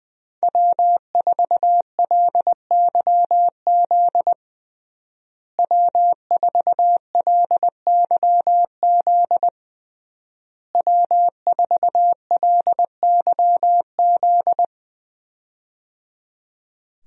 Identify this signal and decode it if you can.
Morse code, heard as a keyed tone, about 21 wpm, 700 Hz. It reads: W4LYZ W4LYZ W4LYZ
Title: USA Call Signs - 5 Characters - Repeated 3x - 20wpm